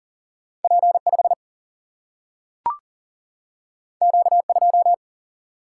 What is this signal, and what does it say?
Morse code, heard as a keyed tone, about 39 wpm, 700 Hz. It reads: P5 Q2
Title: Single-Letter Single-Number 40wpm